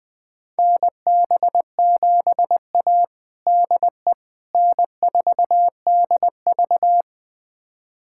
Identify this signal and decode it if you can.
Morse code, heard as a keyed tone, about 20 wpm, 700 Hz. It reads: NB7A DE N4DV
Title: USA Callsign DE Callsign - 4 Characters - 20wpm